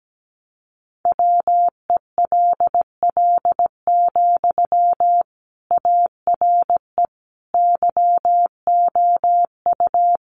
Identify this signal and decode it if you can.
Morse code, heard as a keyed tone, about 17 wpm, 700 Hz. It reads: WELL, ARE YOU
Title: Sentences from Top 400 Words 17wpm